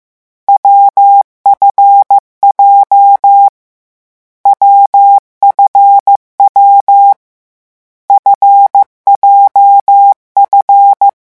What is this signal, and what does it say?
Morse code, heard as a keyed tone, about 15 wpm, 800 Hz. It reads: WFJ WFW FJF
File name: morse mp3 / lecon7-vitesse 14